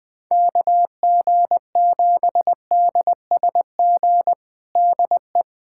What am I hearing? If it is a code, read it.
Morse code, heard as a keyed tone, about 20 wpm, 700 Hz. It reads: KG7DSG DE
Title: USA Callsign DE Callsign - 6 Characters - 20wpm